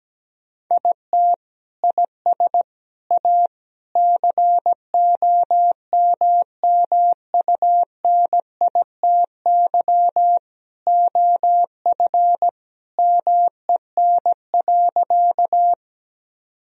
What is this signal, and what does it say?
Morse code, heard as a keyed tone, about 17 wpm, 700 Hz. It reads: IT IS A COMMUNITY OF MEN.
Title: Sentences from Top 1000 Words - Review Part 4 - 17wpm